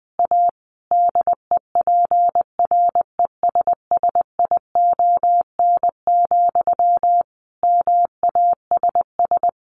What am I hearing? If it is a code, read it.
Morse code, heard as a keyed tone, about 20 wpm, 700 Hz. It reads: A DEPRESSION, MASS
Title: A Sound Of Thunder 20wpm